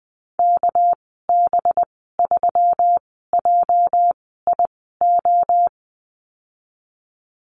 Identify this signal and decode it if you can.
Morse code, heard as a keyed tone, about 20 wpm, spaced as for 14 wpm, 700 Hz. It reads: KB3JIO
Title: 245 Lesson 39g - Letter z - Call Signs 20wpm